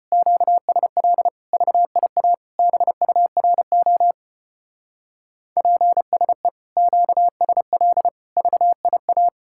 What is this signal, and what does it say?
Morse code, heard as a keyed tone, about 32 wpm, 700 Hz. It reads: QSL VIA BURO PSE QSL VIA